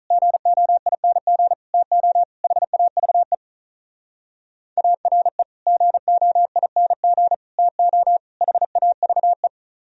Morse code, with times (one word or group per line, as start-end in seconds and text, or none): 0.1-1.5 GOING
1.7-2.2 TO
2.4-3.3 HAVE
4.8-5.4 ARE
5.7-7.3 GOING
7.6-8.2 TO
8.4-9.5 HAVE